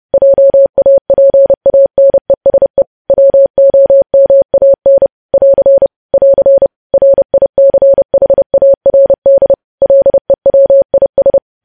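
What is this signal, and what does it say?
Morse code, heard as a keyed tone, about 30 wpm, 550 Hz. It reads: JAPANESE WOMAN <AR> <AR> RICHARD LEWIS
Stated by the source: Morse news headlines sent at 30 wpm.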